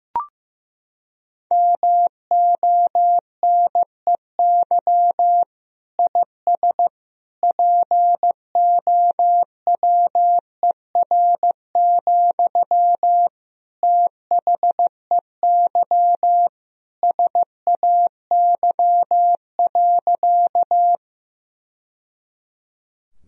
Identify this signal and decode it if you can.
Morse code, heard as a keyed tone, about 16 wpm, 700 Hz. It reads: MONEY IS POWER, THEY SAY.
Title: Sentences from Top 1000 Words - Review Part 2 - 15wpm